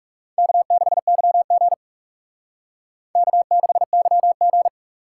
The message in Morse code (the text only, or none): K6YG K6YG